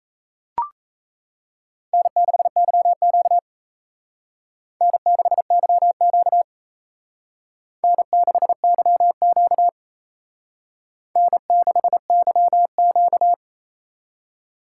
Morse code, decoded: N6YQ N6YQ N6YQ N6YQ

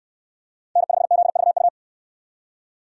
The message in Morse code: N5ZVU

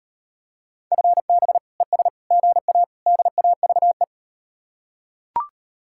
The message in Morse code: FB ES GA DAVE

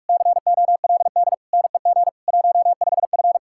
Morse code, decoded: COLD NEG 15F

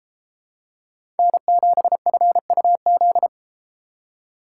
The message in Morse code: N7FUZ

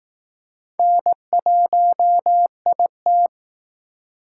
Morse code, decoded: N1IT

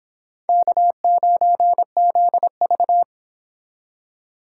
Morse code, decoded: K9ZV

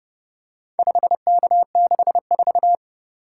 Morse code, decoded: 5K64

